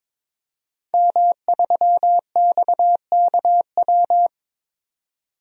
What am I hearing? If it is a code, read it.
Morse code, heard as a keyed tone, about 22 wpm, 700 Hz. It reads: M3XKW
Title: DXCC-100 Call Signs - 5 Characters - 22wpm